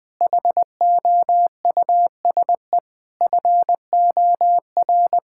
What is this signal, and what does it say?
Morse code, heard as a keyed tone, about 20 wpm, 700 Hz. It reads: HOUSE FOR
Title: Sentences from Top 1000 Words - Review Part 5 - 20wpm